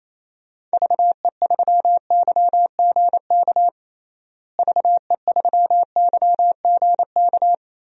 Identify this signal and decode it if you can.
Morse code, heard as a keyed tone, about 28 wpm, 700 Hz. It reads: VE3YGK VE3YGK
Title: DXCC-100 Call Signs - 6 Characters - Repeated 3x - 28wpm